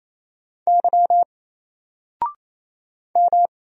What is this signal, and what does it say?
Morse code, heard as a keyed tone, about 28 wpm, 700 Hz. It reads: Y M
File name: Single-Letter-Number-28wpm